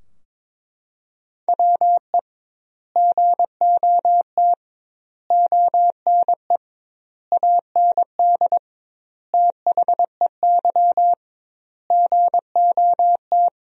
Morse code, heard as a keyed tone, about 22 wpm, 700 Hz. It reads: WE GOT ONE AND THEY GOT
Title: 2x Word-Spacing - Sentences from Top 300 Words - 22wpm